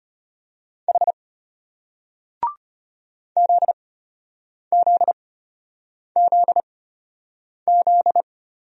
Morse code, heard as a keyed tone, about 33 wpm, spaced as for 8 wpm, 700 Hz. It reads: H ZZZZ